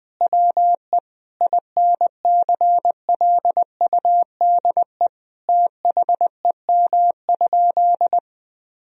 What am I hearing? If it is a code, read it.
Morse code, heard as a keyed tone, about 20 wpm, 700 Hz. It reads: WE INCLUDE THEM?
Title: Sentences from Top 1000 Words - Review Part 3 - 20wpm